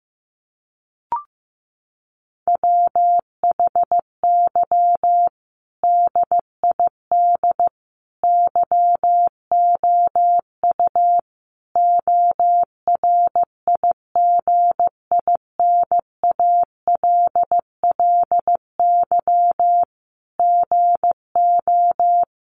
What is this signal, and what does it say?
Morse code, heard as a keyed tone, about 15 wpm, 700 Hz. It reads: WHY DID YOU ORIGINALLY GO
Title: Sentences from Top 1500 Words 15wpm